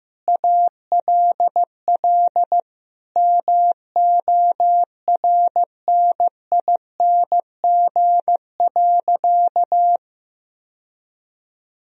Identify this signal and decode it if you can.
Morse code, heard as a keyed tone, about 15 wpm, 700 Hz. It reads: ALL MORNING.
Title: Sentences from Top 1000 Words - Review Part 3 - 15wpm